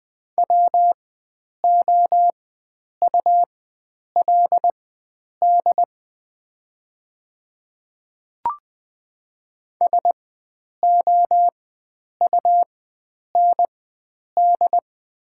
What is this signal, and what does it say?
Morse code, heard as a keyed tone, about 20 wpm, spaced as for 9 wpm, 700 Hz. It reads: WOULD SOUND
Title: 4x Character Spacing - Top 100 Words - 20wpm